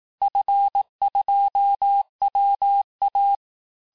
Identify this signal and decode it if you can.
Morse code, heard as a keyed tone, about 18 wpm, 800 Hz. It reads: F2WA
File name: cours-du-lundi files / cours-1594--FIN-de-QSO-vitesse-17mpm